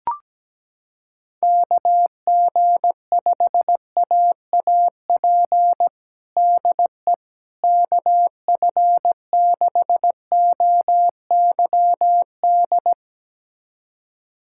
Morse code, heard as a keyed tone, about 17 wpm, 700 Hz. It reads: KG5AAP DE KF6OYD
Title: USA Callsign DE Callsign - 6 Characters - 17wpm